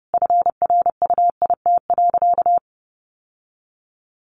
Morse code, heard as a keyed tone, about 30 wpm, 700 Hz. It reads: FRUIT.
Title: Sentences from Top 1300 Words 30wpm